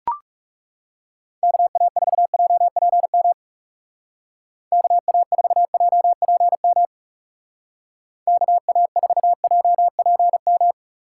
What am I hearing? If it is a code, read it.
Morse code, heard as a keyed tone, about 38 wpm, 700 Hz. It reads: KA4JPM KA4JPM KA4JPM